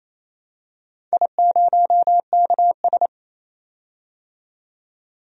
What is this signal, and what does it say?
Morse code, heard as a keyed tone, about 28 wpm, 700 Hz. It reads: I0KS
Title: DXCC-100 Call Signs - 4 Characters - 28wpm